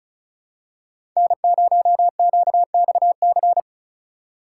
Morse code, heard as a keyed tone, about 34 wpm, 700 Hz. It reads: N0QXC
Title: USA Call Signs - 5 Characters - 6x Character Spacing - 35wpm